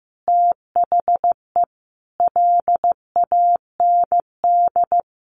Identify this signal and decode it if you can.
Morse code, heard as a keyed tone, about 15 wpm, 700 Hz. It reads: THE LAND
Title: Sentences from Top 1100 Words 15wpm